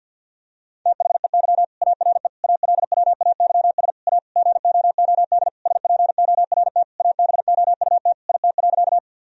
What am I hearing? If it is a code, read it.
Morse code, heard as a keyed tone, about 51 wpm, 700 Hz. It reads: THEY ARE ALWAYS A GOOD SPORT ABOUT IT.
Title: Sentences from Top 1000 Words - Review Part 4 - 50wpm